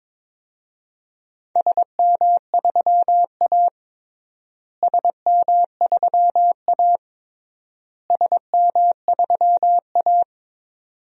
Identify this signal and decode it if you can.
Morse code, heard as a keyed tone, about 22 wpm, 700 Hz. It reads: SM3A SM3A SM3A
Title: DXCC-100 Call Signs - 4 Characters - Repeated 3x - 22wpm